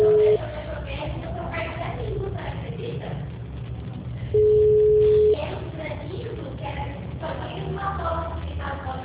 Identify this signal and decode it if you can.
Ringback tone (Europe, 425 Hz), with speech.